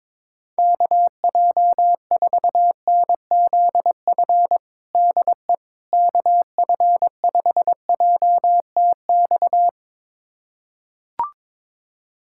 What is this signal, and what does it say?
Morse code, heard as a keyed tone, about 22 wpm, 700 Hz. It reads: KJ4NZF DE KF5JTX
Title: USA Callsign DE Callsign - 6 Characters - 22wpm